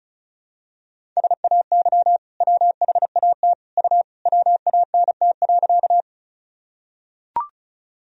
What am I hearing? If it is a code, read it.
Morse code, heard as a keyed tone, about 36 wpm, 700 Hz. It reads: SAY WHAT U WANT.
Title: Sentences from Top 200 Words and Abbreviations 35wpm